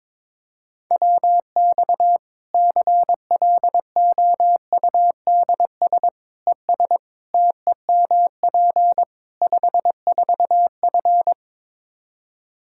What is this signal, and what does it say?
Morse code, heard as a keyed tone, about 22 wpm, 700 Hz. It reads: WX CLOUDS ES TEMP 54F
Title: Practice QSO - Exchange 2 - Info - 22wpm